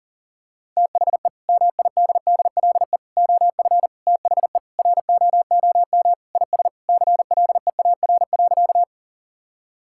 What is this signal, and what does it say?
Morse code, heard as a keyed tone, about 40 wpm, 700 Hz. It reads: THE MIDDLE OF THE ROOM IS CLEAR.
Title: Sentences from Top 1000 Words - Review Part 6 - 40wpm